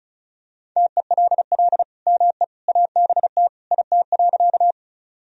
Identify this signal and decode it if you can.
Morse code, heard as a keyed tone, about 34 wpm, 700 Hz. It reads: TELL ME ABT IT.
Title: Sentences from Top 200 Words and Abbreviations 35wpm